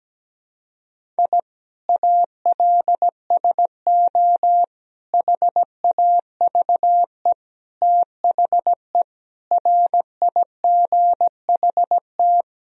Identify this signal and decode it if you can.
Morse code, heard as a keyed tone, about 17 wpm, 700 Hz. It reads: I ALSO HAVE THE RIGHT
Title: Sentences from Top 1000 Words - Review Part 1 - 17wpm